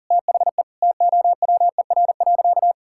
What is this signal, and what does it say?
Morse code, heard as a keyed tone, about 37 wpm, 700 Hz. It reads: THE TOWER.
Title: Pangram Challenge v2 - 40wpm